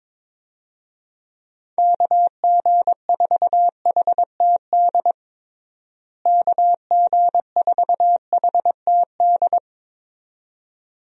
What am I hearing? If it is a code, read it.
Morse code, heard as a keyed tone, about 22 wpm, 700 Hz. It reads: KG4HTD KG4HTD
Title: USA Call Signs - 6 Characters - Repeated 2x - 22wpm